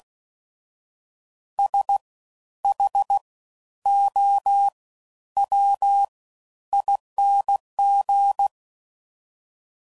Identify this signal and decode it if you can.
Morse code, heard as a keyed tone, about 16 wpm, spaced as for 10 wpm, 800 Hz. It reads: SHOWING
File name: Suffixes / Special INGs / Fast Ings